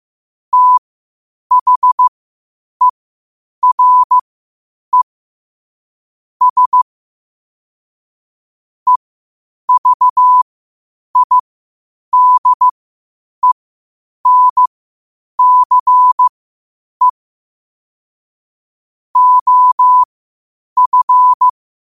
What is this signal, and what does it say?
Morse code, heard as a keyed tone, about 15 wpm, spaced as for 8 wpm, 1000 Hz. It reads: THERE S EVIDENCE OF
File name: qotdfiles / 8wpm